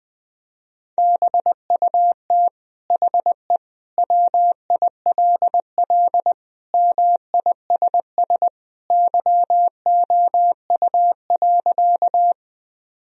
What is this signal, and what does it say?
Morse code, heard as a keyed tone, about 20 wpm, 700 Hz. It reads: BUT HE WILL MISS YOU.
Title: Sentences from Top 1000 Words - Review Part 3 - 20wpm